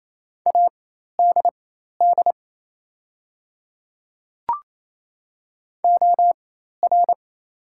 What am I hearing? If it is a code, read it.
Morse code, heard as a keyed tone, about 28 wpm, spaced as for 13 wpm, 700 Hz. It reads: ADD OR
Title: 4x Character Spacing - Top 500 Words - 28wpm